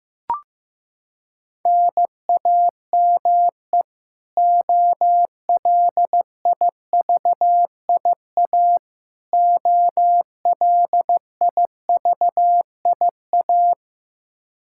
Morse code, heard as a keyed tone, about 15 wpm, 700 Hz. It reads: NAME OLIVIA OLIVIA
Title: Practice QSO - Exchange 1 - Beginning - 15wpm